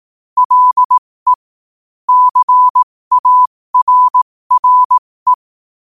Morse code, heard as a keyed tone, about 18 wpm, spaced as for 15 wpm, 1000 Hz. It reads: LE CARRE
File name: qotdfiles / 15wpm